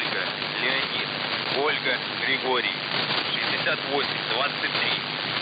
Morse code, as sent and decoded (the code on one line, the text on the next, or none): none